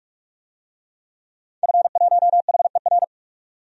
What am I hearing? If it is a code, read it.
Morse code, heard as a keyed tone, about 41 wpm, 700 Hz. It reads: F1HER
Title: DXCC-100 Call Signs - 5 Characters - 45wpm